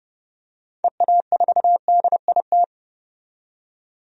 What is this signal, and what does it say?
Morse code, heard as a keyed tone, about 30 wpm, 700 Hz. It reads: EA4DIT
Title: DXCC-100 Call Signs - 6 Characters - 30wpm